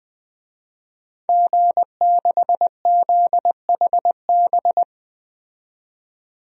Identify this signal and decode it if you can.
Morse code, heard as a keyed tone, about 20 wpm, 700 Hz. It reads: G6ZHB
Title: DXCC-100 Call Signs - 5 Characters - 20wpm